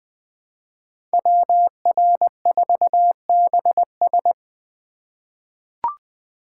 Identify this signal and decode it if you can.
Morse code, heard as a keyed tone, about 20 wpm, 700 Hz. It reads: WR4BS
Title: USA Call Signs - 5 Characters - 4x Character Spacing - 20wpm